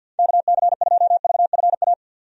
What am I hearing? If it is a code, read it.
Morse code, heard as a keyed tone, about 52 wpm, 700 Hz. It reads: KC2VFA
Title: USA Call Signs - 6 Characters - 50wpm